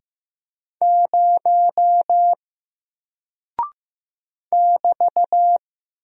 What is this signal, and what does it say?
Morse code, heard as a keyed tone, about 15 wpm, 700 Hz. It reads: 0 <BT>
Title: Single Character (Letter:Number:Punctuation:Prosign) Alt 15wpm